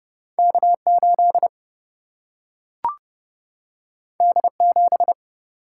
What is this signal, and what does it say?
Morse code, heard as a keyed tone, about 30 wpm, 700 Hz. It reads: K 8 D 7
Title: Single-Letter Single-Number 30wpm